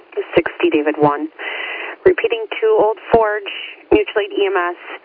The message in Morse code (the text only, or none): none